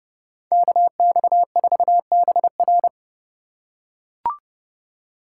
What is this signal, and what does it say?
Morse code, heard as a keyed tone, about 29 wpm, 700 Hz. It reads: KX4BR